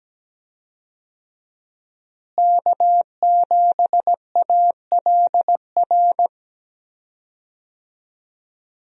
Morse code, heard as a keyed tone, about 17 wpm, 700 Hz. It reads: K7ALR
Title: USA Call Signs - 5 Characters - Repeated 2x - 17wpm